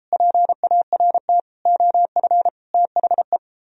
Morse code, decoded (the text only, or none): PART OF THE